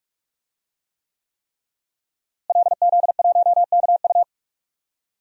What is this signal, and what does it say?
Morse code, heard as a keyed tone, about 46 wpm, 700 Hz. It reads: LZ1KU
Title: DXCC-100 Call Signs - 5 Characters - Repeated 3x - 45wpm